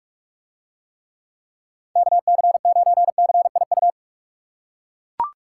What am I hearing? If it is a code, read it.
Morse code, heard as a keyed tone, about 41 wpm, 700 Hz. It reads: KC9CIU